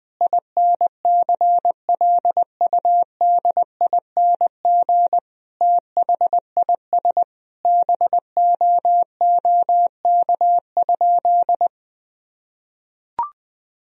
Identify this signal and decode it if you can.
Morse code, heard as a keyed tone, about 20 wpm, 700 Hz. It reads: INCLUDING THIS BOOK?
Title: Sentences from Top 1000 Words - Review Part 4 - 20wpm